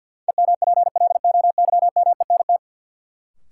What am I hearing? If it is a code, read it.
Morse code, heard as a keyed tone, about 53 wpm, 700 Hz. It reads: EMPLOYMENT
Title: Sentences from Top 1500 Words - New Word Drill - 50wpm